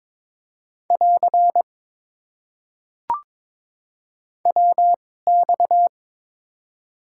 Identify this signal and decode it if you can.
Morse code, heard as a keyed tone, about 21 wpm, spaced as for 16 wpm, 700 Hz. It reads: <AR> WX
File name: Top 90 QSO Elements - 2x Character Spacing - 22wpm